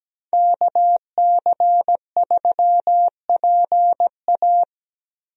Morse code, heard as a keyed tone, about 17 wpm, 700 Hz. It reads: KC3PA
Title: USA Call Signs - 5 Characters - Repeated 3x - 17wpm